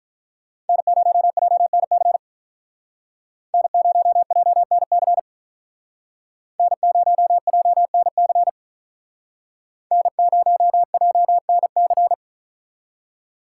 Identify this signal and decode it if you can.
Morse code, heard as a keyed tone, about 44 wpm, 700 Hz. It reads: N0JNC N0JNC N0JNC N0JNC
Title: Speed Racing - USA Call Signs 35wpm